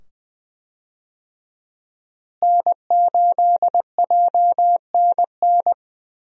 Morse code, heard as a keyed tone, about 20 wpm, 700 Hz. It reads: N8JNN
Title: USA Call Signs - 5 Characters - Repeated 2x - 20wpm